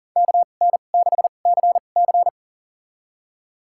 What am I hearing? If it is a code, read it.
Morse code, heard as a keyed tone, about 40 wpm, spaced as for 29 wpm, 700 Hz. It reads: KN6CC
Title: USA Call Signs - 5 Characters - 2x Character Spacing - 40wpm